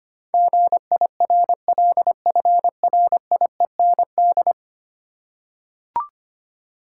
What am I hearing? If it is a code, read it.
Morse code, heard as a keyed tone, about 25 wpm, 700 Hz. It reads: GIRLFRIEND
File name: 10 Letter Words - First 3 Letters Given - 25wpm